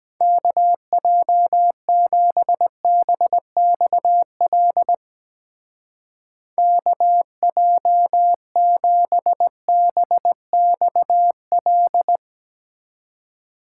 Morse code, heard as a keyed tone, about 21 wpm, 700 Hz. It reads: KJ7BXL KJ7BXL
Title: Speed Racing - USA 2x3 Call Signs 17wpm